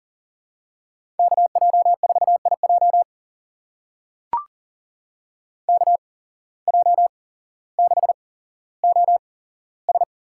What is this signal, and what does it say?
Morse code, heard as a keyed tone, about 39 wpm, spaced as for 17 wpm, 700 Hz. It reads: KJ4IJ KJ6OS
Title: USA Call Signs - 5 Characters - 8x Character Spacing - 40wpm